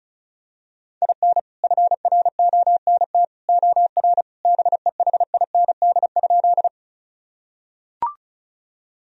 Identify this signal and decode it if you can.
Morse code, heard as a keyed tone, about 34 wpm, 700 Hz. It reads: IN FRONT OR BEHIND?
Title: Sentences from Top 1000 Words - Review Part 2 - 35wpm